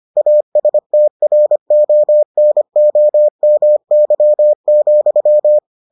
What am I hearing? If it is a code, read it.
Morse code, heard as a keyed tone, about 25 wpm, 600 Hz. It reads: ASTRONOMY,